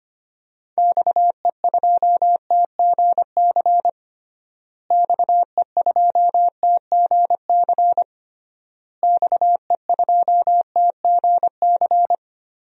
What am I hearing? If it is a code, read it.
Morse code, heard as a keyed tone, about 25 wpm, 700 Hz. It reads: XE2TGC XE2TGC XE2TGC
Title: DXCC-100 Call Signs - 6 Characters - Repeated 3x - 25wpm